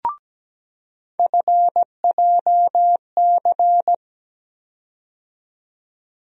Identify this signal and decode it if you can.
Morse code, heard as a keyed tone, about 17 wpm, 700 Hz. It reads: FJC
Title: USA Call Sign Postfixes - 3 Letters 17wpm